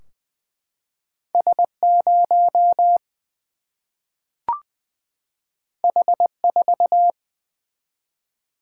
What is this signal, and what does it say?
Morse code, heard as a keyed tone, about 20 wpm, 700 Hz. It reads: S 0 H 4
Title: Single-Letter Single-Number 20wpm